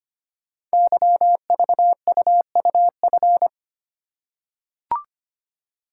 Morse code, heard as a keyed tone, about 25 wpm, 700 Hz. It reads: YVUUF
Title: Code Groups - 5 Letter:Numbers - 25wpm